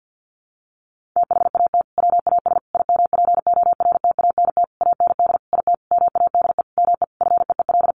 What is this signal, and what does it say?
Morse code, heard as a keyed tone, about 50 wpm, 700 Hz. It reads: THAT WAS IMPORTANT AND IT MADE ME FEEL
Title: Sentences from Top 600 Words 50wpm